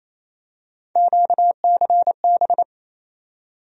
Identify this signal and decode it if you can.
Morse code, heard as a keyed tone, about 28 wpm, 700 Hz. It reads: QCB
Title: USA Call Sign Postfixes - 3 Letters 28wpm